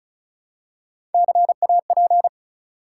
Morse code, heard as a keyed tone, about 36 wpm, 700 Hz. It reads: CAP
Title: Three Letter Words - 35wpm